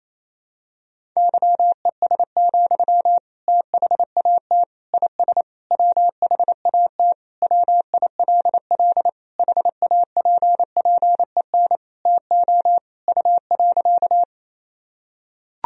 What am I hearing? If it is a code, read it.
Morse code, heard as a keyed tone, about 28 wpm, 700 Hz. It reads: YES, THAT IS WHAT WILL HAPPEN TO U.
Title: Sentences from Top 500 Words and Abbreviations 28wpm